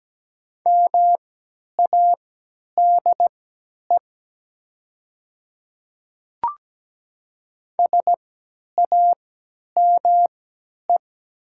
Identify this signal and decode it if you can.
Morse code, heard as a keyed tone, about 17 wpm, spaced as for 10 wpm, 700 Hz. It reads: MADE SAME
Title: 3x Character Spacing - Top 200 Words - 17wpm